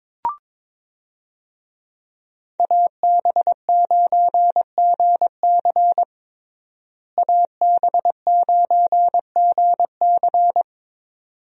Morse code, heard as a keyed tone, about 22 wpm, 700 Hz. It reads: AB9GC AB9GC
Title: USA Call Signs - 5 Characters - Repeated 2x - 22wpm